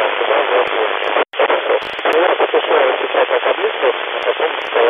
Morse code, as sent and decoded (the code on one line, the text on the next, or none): none